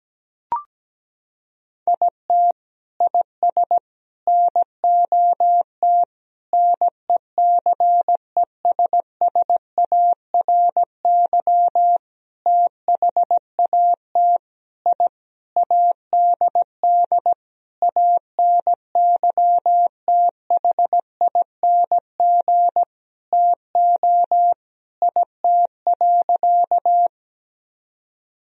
Morse code, heard as a keyed tone, about 17 wpm, 700 Hz. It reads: IT IS NOT NECESSARY THAT I ADD ANYTHING TO IT.
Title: Sentences from Top 1000 Words - Review Part 1 - 17wpm